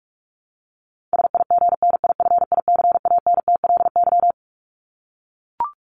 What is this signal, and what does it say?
Morse code, heard as a keyed tone, about 44 wpm, 700 Hz. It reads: SIGNIFICANTLY
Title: Top 1251-1500 Words 45wpm